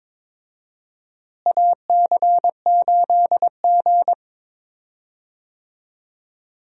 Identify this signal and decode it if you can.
Morse code, heard as a keyed tone, about 22 wpm, 700 Hz. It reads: AC8G
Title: USA Call Signs - 4 Characters - Repeated 2x - 22wpm